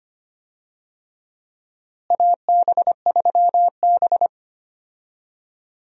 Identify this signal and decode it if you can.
Morse code, heard as a keyed tone, about 25 wpm, 700 Hz. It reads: AB3B
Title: USA Call Signs - 4 Characters - Repeated 3x - 25wpm